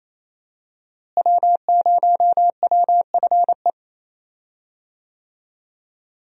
Morse code, heard as a keyed tone, about 28 wpm, 700 Hz. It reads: W0WFE
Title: USA Call Signs - 5 Characters - Repeated 3x - 28wpm